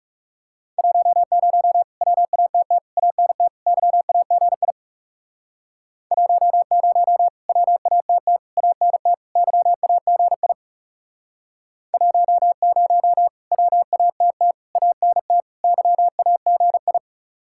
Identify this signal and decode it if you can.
Morse code, heard as a keyed tone, about 37 wpm, 700 Hz. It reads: 10 WATT ANT YAGI 10 WATT ANT YAGI 10 WATT ANT YAGI
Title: Speed-Racing Common QSO Phrases - 30wpm